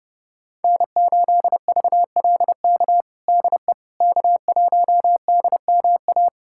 Morse code, heard as a keyed tone, about 30 wpm, 700 Hz. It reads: N8VLK DE K1DMA
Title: USA Callsign DE Callsign - 5 Characters - 30wpm